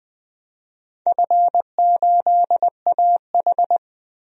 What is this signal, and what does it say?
Morse code, heard as a keyed tone, about 20 wpm, 700 Hz. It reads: F8AH
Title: DXCC-100 Call Signs - 4 Characters - 20wpm